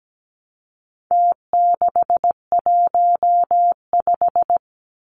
Morse code, heard as a keyed tone, about 17 wpm, 700 Hz. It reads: T615
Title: Numbers - Contest Style - 4-Digit 17wpm